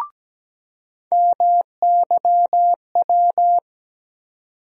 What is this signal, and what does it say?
Morse code, heard as a keyed tone, about 17 wpm, 700 Hz. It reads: MYW
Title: USA Call Sign Postfixes - 3 Letters 17wpm